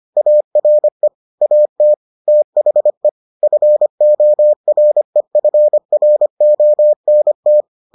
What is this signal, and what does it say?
Morse code, heard as a keyed tone, about 25 wpm, 600 Hz. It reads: ARE AT THE FOREFRONT